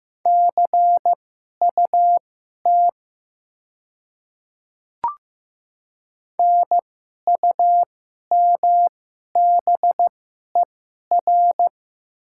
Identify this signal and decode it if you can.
Morse code, heard as a keyed tone, about 15 wpm, spaced as for 11 wpm, 700 Hz. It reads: CUT NUMBER
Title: 2x Character Spacing - Top 500 Words - 15wpm